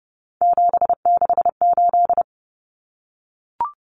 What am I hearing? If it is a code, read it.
Morse code, heard as a keyed tone, about 30 wpm, 700 Hz. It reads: 768
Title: Common Cut Numbers - 3-Digit 30wpm